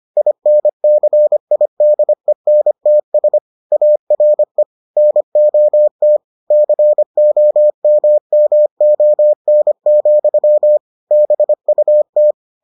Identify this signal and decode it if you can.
Morse code, heard as a keyed tone, about 25 wpm, 600 Hz. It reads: INCIDENTS ARE NOT COMMON, BUT